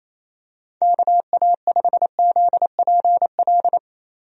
Morse code, heard as a keyed tone, about 28 wpm, 700 Hz. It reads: KA5ZPL
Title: USA Call Signs - 6 Characters - Repeated 2x - 28wpm